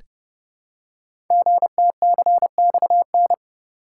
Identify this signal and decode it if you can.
Morse code, heard as a keyed tone, about 29 wpm, 700 Hz. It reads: GTCXN
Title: Code Groups - 5 Letters - 30wpm